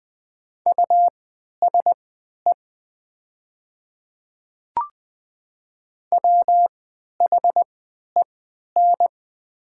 Morse code, heard as a keyed tone, about 20 wpm, spaced as for 11 wpm, 700 Hz. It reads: USE WHEN
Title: 3x Character Spacing - Top 200 Words - 20wpm